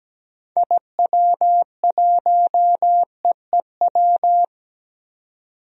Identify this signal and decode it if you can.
Morse code, heard as a keyed tone, about 17 wpm, 700 Hz. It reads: IW1EEW
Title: DXCC-100 Call Signs - 6 Characters - 17wpm